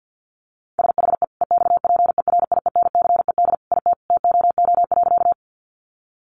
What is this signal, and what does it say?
Morse code, heard as a keyed tone, about 50 wpm, 700 Hz. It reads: SHE EXPERIENCED IT TOO.
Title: Sentences from Top 1100 Words 50wpm